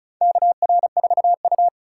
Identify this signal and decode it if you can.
Morse code, heard as a keyed tone, about 36 wpm, 700 Hz. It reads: KR4U
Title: USA Call Signs - 4 Characters - 4x Character Spacing - 35wpm